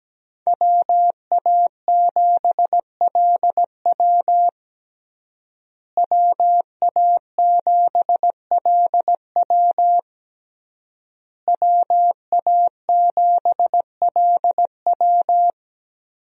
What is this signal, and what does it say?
Morse code, heard as a keyed tone, about 17 wpm, 700 Hz. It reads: WA7LW WA7LW WA7LW
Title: USA Call Signs - 5 Characters - Repeated 3x - 17wpm